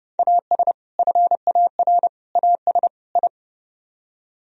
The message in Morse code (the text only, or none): AS FAR AS I